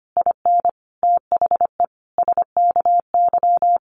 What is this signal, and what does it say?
Morse code, heard as a keyed tone, about 25 wpm, 700 Hz. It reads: IN THE SKY
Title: Sentences from Top 1300 Words 25wpm